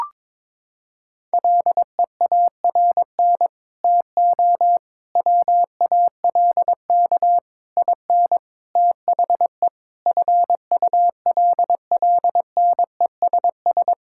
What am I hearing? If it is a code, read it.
Morse code, heard as a keyed tone, about 22 wpm, 700 Hz. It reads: LEARN TO WALK IN THE FULLNESS
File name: Sets of 9 Words - 22wpm